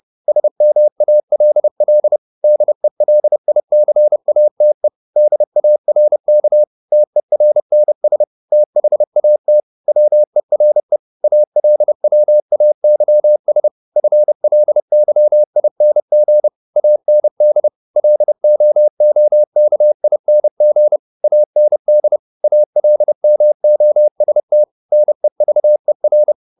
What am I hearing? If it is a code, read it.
Morse code, heard as a keyed tone, about 30 wpm, 600 Hz. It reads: SMALL DELICATE DARK TERNS THAT WERE ALWAYS FLYING AND LOOKING AND ALMOST NEVER
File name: the old man and the sea / the old man and the sea 1 30wpm